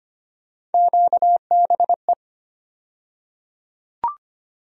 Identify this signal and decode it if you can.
Morse code, heard as a keyed tone, about 25 wpm, 700 Hz. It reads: QBE
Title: USA Call Sign Postfixes - 3 Letters 25wpm